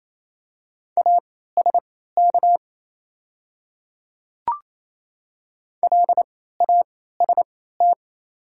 Morse code, heard as a keyed tone, about 27 wpm, spaced as for 16 wpm, 690 Hz. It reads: ASK LAST
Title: 3x Character Spacing - Top 500 Words - 28wpm